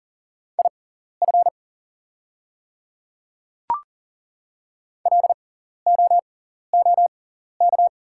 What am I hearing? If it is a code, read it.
Morse code, heard as a keyed tone, about 40 wpm, spaced as for 14 wpm, 700 Hz. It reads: IF LOOK
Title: 6x Character Spacing - Top 100 Words - 40wpm